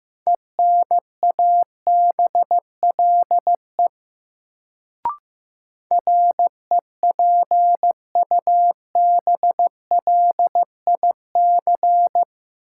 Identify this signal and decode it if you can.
Morse code, heard as a keyed tone, about 15 wpm, 700 Hz. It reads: ENABLE REPUBLIC
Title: Sentences from Top 1500 Words - New Word Drill - 15wpm